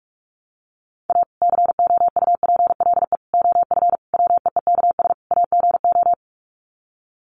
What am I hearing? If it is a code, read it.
Morse code, heard as a keyed tone, about 44 wpm, 700 Hz. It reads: A COUPLE OF WEEKS AGO
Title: Sets of 5 Words 45wpm